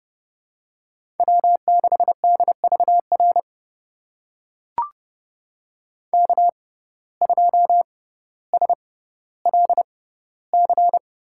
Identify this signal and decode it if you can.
Morse code, heard as a keyed tone, about 32 wpm, spaced as for 16 wpm, 700 Hz. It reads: W6DVR K2SLC
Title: USA Call Signs - 5 Characters - 6x Character Spacing - 30wpm